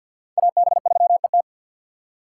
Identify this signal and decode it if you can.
Morse code, heard as a keyed tone, about 50 wpm, 690 Hz. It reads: AB3ET
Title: USA Call Signs - 5 Characters - 3x Character Spacing - 50wpm